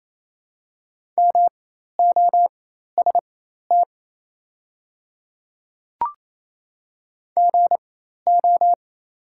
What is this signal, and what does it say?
Morse code, heard as a keyed tone, about 28 wpm, spaced as for 13 wpm, 700 Hz. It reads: MOST GO
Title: 4x Character Spacing - Top 100 Words - 28wpm